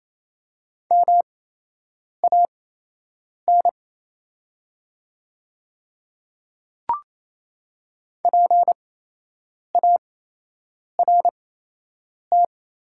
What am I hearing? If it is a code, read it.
Morse code, heard as a keyed tone, about 28 wpm, spaced as for 8 wpm, 700 Hz. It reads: MAN PART
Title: 8x Character Spacing - Top 200 Words - 28wpm